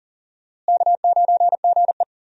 Morse code, heard as a keyed tone, about 40 wpm, 700 Hz. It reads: K9GE
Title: USA Call Signs - 4 Characters - Repeated 3x - 40wpm